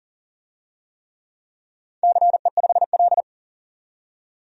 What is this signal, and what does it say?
Morse code, heard as a keyed tone, about 40 wpm, 700 Hz. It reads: CE5L
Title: DXCC-100 Call Signs - 4 Characters - 40wpm